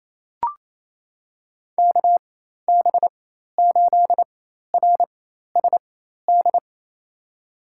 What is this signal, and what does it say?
Morse code, heard as a keyed tone, about 28 wpm, spaced as for 13 wpm, 700 Hz. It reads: KB8RSD
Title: USA Call Signs - 6 Characters - 4x Character Spacing - 28wpm